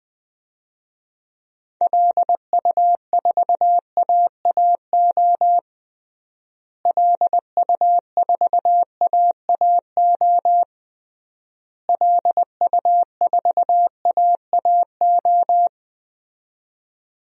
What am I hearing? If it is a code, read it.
Morse code, heard as a keyed tone, about 21 wpm, 700 Hz. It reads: LU4AAO LU4AAO LU4AAO
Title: DXCC-100 Call Signs - 6 Characters - Repeated 3x - 20wpm